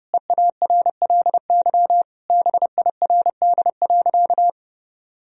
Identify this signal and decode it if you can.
Morse code, heard as a keyed tone, about 32 wpm, 700 Hz. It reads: EARLY BIRD.
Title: Sentences from Top 1000 Words - Review Part 2 - 30wpm